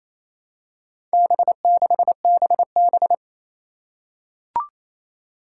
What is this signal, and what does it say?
Morse code, heard as a keyed tone, about 28 wpm, 700 Hz. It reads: B6BB
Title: Four Challenging Letter-Number Combinations 28wpm